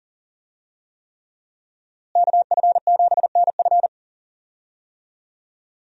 Morse code, heard as a keyed tone, about 40 wpm, 700 Hz. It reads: KF7NF
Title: USA Call Signs - 5 Characters - Repeated 3x - 40wpm